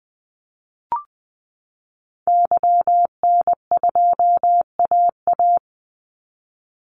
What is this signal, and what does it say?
Morse code, heard as a keyed tone, about 20 wpm, 700 Hz. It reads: YN2AA
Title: Call Signs Repeated 3x - 20wpm